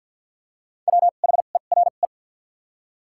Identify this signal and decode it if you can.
Morse code, heard as a keyed tone, about 49 wpm, spaced as for 36 wpm, 700 Hz. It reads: WHERE
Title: 2x Character Spacing - Top 300 Words - 50wpm